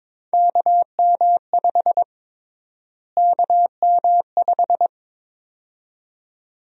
Morse code, heard as a keyed tone, about 22 wpm, 700 Hz. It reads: KM5 KM5
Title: USA Call Sign Prefixes - Repeated 2x - 22wpm